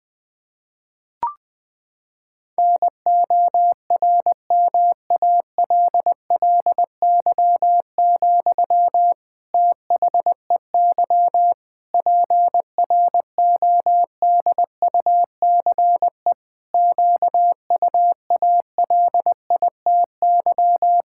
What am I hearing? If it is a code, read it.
Morse code, heard as a keyed tone, about 21 wpm, 700 Hz. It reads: NORMALLY, THEY PRODUCE QUALITY
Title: Sentences from Top 1000 Words - Review Part 6 - 20wpm